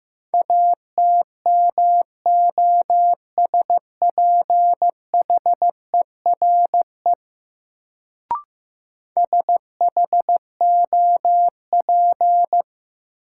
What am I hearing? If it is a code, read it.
Morse code, heard as a keyed tone, about 15 wpm, 700 Hz. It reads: ATMOSPHERE SHOP
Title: Sentences from Top 1600 Words - New Word Drill Alt - 15wpm